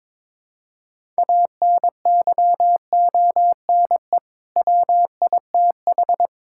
Morse code, heard as a keyed tone, about 22 wpm, 700 Hz. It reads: ANYONE WITH
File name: practice / 22wpm / Sets of 7 Words - 22wpm